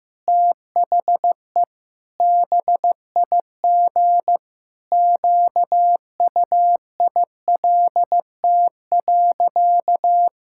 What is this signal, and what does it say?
Morse code, heard as a keyed tone, about 16 wpm, 700 Hz. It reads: THE BIG QUILT.
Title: Pangram Challenge v2 - 15wpm